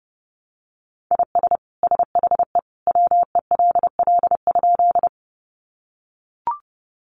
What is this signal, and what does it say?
Morse code, heard as a keyed tone, about 30 wpm, 700 Hz. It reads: IS SHE WELL?
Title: Sentences from Top 200 Words 30wpm